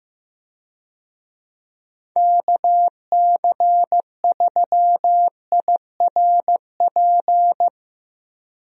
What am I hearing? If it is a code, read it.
Morse code, heard as a keyed tone, about 15 wpm, 700 Hz. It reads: KC3IRP
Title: USA Call Signs - 6 Characters - Repeated 3x - 15wpm